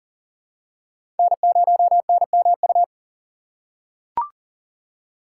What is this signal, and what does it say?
Morse code, heard as a keyed tone, about 40 wpm, 700 Hz. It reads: N0NMU
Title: USA Call Signs - 6x Character Spacing - 40wpm